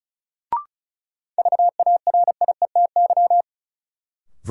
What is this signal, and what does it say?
Morse code, heard as a keyed tone, about 36 wpm, 700 Hz. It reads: VARIETY
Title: Sentences from Top 1000 Words - Review Part 4 - New Word Drill - 35wpm